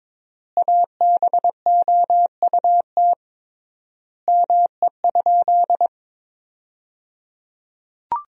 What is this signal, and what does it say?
Morse code, heard as a keyed tone, about 22 wpm, 700 Hz. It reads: ABOUT ME?
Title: 3x Word-Spacing - Sentences from Top 200 Words - 22wpm